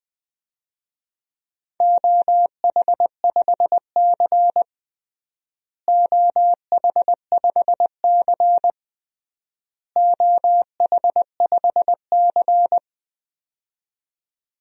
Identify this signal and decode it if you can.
Morse code, heard as a keyed tone, about 20 wpm, 700 Hz. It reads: OH5C OH5C OH5C
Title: DXCC-100 Call Signs - 4 Characters - Repeated 3x - 20wpm